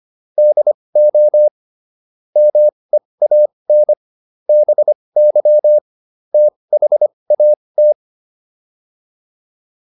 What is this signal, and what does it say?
Morse code, heard as a keyed tone, about 25 wpm, spaced as for 20 wpm, 600 Hz. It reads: DO MEAN BY THAT